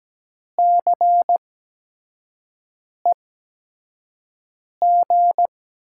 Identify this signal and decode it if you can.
Morse code, heard as a keyed tone, about 17 wpm, spaced as for 5 wpm, 700 Hz. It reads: CEG
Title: USA Call Signs - 8x Character Spacing - 17wpm